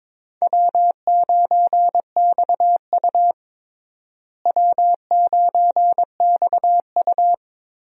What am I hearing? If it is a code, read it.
Morse code, heard as a keyed tone, about 22 wpm, 700 Hz. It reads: W9XU W9XU
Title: USA Call Signs - 4 Characters - Repeated 2x - 22wpm